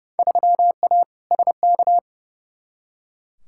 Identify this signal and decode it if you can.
Morse code, heard as a keyed tone, about 31 wpm, 700 Hz. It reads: 3A SK
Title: Field Day Exchanges Encore - 30wpm